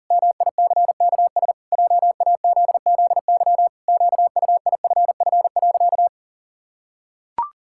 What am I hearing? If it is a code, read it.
Morse code, heard as a keyed tone, about 40 wpm, 700 Hz. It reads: MICKS JAZZY QUIFF.
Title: Pangram Challenge v2 - 40wpm